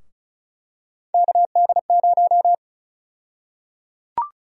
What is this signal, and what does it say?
Morse code, heard as a keyed tone, about 36 wpm, 700 Hz. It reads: KD0
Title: USA Call Sign Prefixes - Repeated 2x - 35wpm